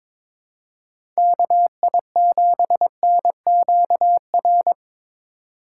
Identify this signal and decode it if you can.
Morse code, heard as a keyed tone, about 22 wpm, 700 Hz. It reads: KI7NQR
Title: USA Call Signs - 6 Characters - 3x Character Spacing - 22wpm